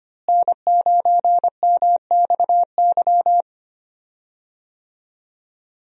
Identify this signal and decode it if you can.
Morse code, heard as a keyed tone, about 25 wpm, 700 Hz. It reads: N9MXY
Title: USA Call Signs - 5 Characters - Repeated 3x - 25wpm